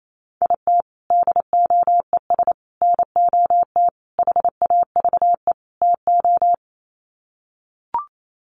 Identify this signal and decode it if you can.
Morse code, heard as a keyed tone, about 28 wpm, 700 Hz. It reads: IT DOES NOT HAVE TO
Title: Sets of 5 Words 28wpm